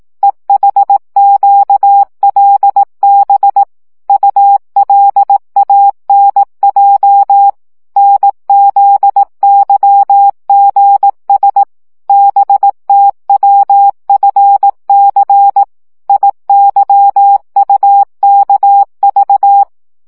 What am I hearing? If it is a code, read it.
Morse code, heard as a keyed tone, about 18 wpm, 800 Hz. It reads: EHQLB ULANJ NZYGS BTWFC IYUKV
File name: morze vizsga 2011-1 15